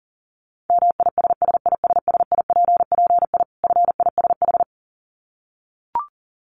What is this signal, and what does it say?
Morse code, heard as a keyed tone, about 40 wpm, 700 Hz. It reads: MISSISSIPPI FISH
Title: Dit Challenge 40wpm